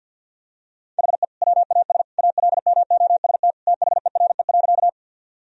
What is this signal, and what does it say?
Morse code, heard as a keyed tone, about 50 wpm, 700 Hz. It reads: HE WAS ALMOST THERE.
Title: Sentences from Top 1000 Words - Review Part 4 - 50wpm